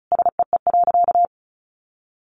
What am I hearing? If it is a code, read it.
Morse code, heard as a keyed tone, about 34 wpm, 700 Hz. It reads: SEE.
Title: Sentences from Top 600 Words 35wpm